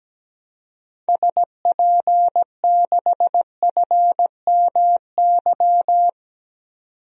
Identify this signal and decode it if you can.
Morse code, heard as a keyed tone, about 17 wpm, 700 Hz. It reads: SP6FMY
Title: DXCC-100 Call Signs - 6 Characters - 17wpm